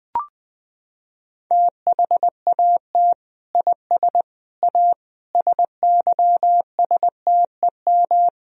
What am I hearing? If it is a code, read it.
Morse code, heard as a keyed tone, about 20 wpm, 700 Hz. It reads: THAT IS A SYSTEM
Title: Sentences from Top 1000 Words - Review Part 3 - 20wpm